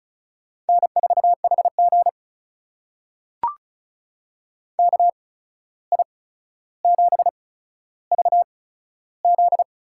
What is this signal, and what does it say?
Morse code, heard as a keyed tone, about 34 wpm, spaced as for 14 wpm, 700 Hz. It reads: N4HG KI7VZ
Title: USA Call Signs - 8x Character Spacing - 35wpm